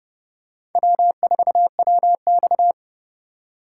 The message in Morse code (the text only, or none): W4WX